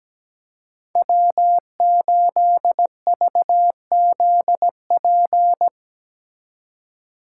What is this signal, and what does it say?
Morse code, heard as a keyed tone, about 20 wpm, 700 Hz. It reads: W8VZP
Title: Speed Racing - USA 1x3 Call Signs 17wpm